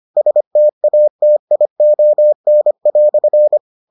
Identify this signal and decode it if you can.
Morse code, heard as a keyed tone, about 25 wpm, 600 Hz. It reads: STATION"